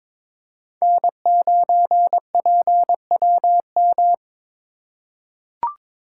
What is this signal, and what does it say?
Morse code, heard as a keyed tone, about 22 wpm, 700 Hz. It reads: N9PWM
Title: USA Call Signs - 3x Character Spacing - 22wpm